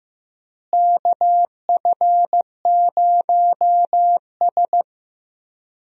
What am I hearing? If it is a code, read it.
Morse code, heard as a keyed tone, about 15 wpm, 700 Hz. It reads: KF0S
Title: USA Call Signs - 4 Characters - Repeated 3x - 15wpm